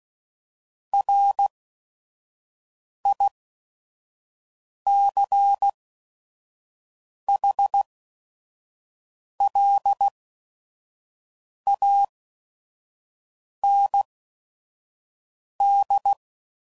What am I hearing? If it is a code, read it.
Morse code, heard as a keyed tone, about 16 wpm, spaced as for 5 wpm, 800 Hz. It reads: RICHLAND
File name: CW Code Groups @5WPM / fromSM's / Cities